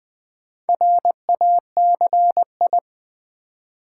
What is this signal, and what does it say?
Morse code, heard as a keyed tone, about 20 wpm, 700 Hz. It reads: RACI
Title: Build A Word - 2 Letters At A Time - 6-Letter Words - 20wpm